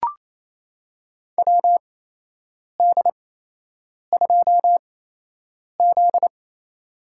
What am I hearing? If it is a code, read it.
Morse code, heard as a keyed tone, about 28 wpm, spaced as for 8 wpm, 700 Hz. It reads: WD2Z
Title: USA Call Signs - 4 Characters - 8x Character Spacing - 28wpm